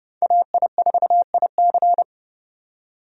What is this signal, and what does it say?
Morse code, heard as a keyed tone, about 30 wpm, 700 Hz. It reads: AI4IC
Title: USA Call Signs - 5 Characters - 30wpm